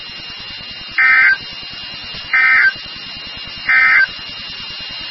Emergency Alert System (SAME) tones, heard as harsh buzzing bursts. It is heard from 0.9 to 4.1 s.
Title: NOAA Weather Radio WXK36